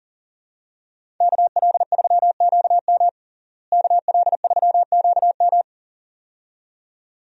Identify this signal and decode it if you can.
Morse code, heard as a keyed tone, about 37 wpm, 700 Hz. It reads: KL3QM KL3QM
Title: USA Call Signs - 5 Characters - Repeated 2x - 40wpm